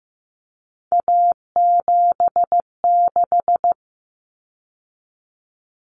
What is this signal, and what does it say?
Morse code, heard as a keyed tone, about 15 wpm, 700 Hz. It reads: A76
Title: Common Cut Numbers - 3-Digit 15wpm